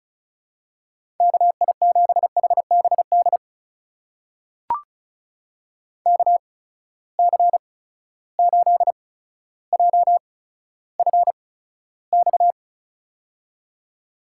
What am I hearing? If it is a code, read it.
Morse code, heard as a keyed tone, about 36 wpm, spaced as for 15 wpm, 700 Hz. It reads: KI7HBD KC8JFX
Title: USA Call Signs - 6 Characters - 8x Character Spacing - 35wpm